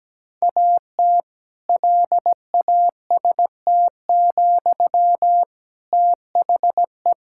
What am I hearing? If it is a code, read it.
Morse code, heard as a keyed tone, about 17 wpm, 700 Hz. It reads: AT LAST, THE
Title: Sentences from Top 1000 Words - Review Part 6 - 17wpm